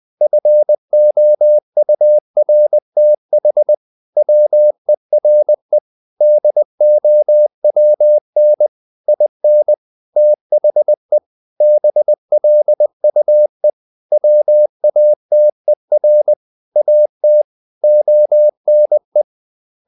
Morse code, heard as a keyed tone, about 20 wpm, 600 Hz. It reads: FOURTH WERE DOWN IN THE BLUE WATER AT ONE
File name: the old man and the sea / the old man and the sea part 1 20wpm